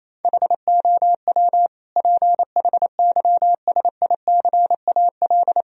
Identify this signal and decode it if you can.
Morse code, heard as a keyed tone, about 27 wpm, 700 Hz. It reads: HOW PHYSICAL
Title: Sentences from Top 1000 Words - Review Part 4 - 28wpm